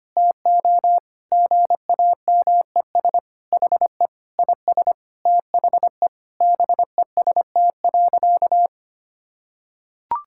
Morse code, heard as a keyed tone, about 25 wpm, 700 Hz. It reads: TO GAMES HE IS THE BEST.
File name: practice / 25wpm / Sentences with 3-Word Phrases - 25wpm